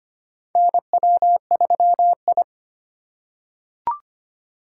Morse code, heard as a keyed tone, about 25 wpm, 700 Hz. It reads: NW3I
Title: USA Call Signs - 4 Characters - 3x Character Spacing - 25wpm